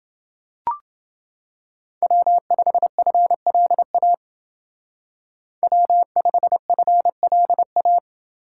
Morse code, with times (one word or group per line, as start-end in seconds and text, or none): 2.0-4.1 W5FLA
5.6-8.0 W5FLA